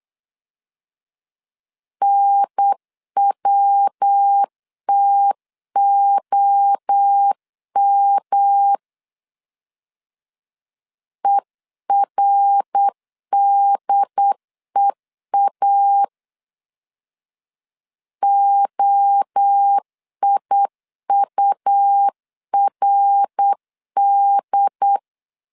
Morse code, heard as a keyed tone, about 8 wpm, 800 Hz. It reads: NWTOM ERDEA OIURD
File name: CW / DET / 1 3 1500C S C 8wpm 4214